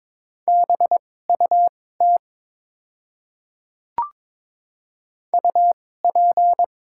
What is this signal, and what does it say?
Morse code, heard as a keyed tone, about 22 wpm, spaced as for 16 wpm, 700 Hz. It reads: BUT UP
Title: 2x Character Spacing - Top 100 Words - 22wpm